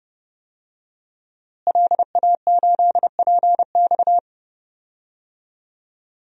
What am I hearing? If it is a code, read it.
Morse code, heard as a keyed tone, about 30 wpm, 700 Hz. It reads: LA8PX
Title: DXCC-100 Call Signs - 5 Characters - Repeated 3x - 30wpm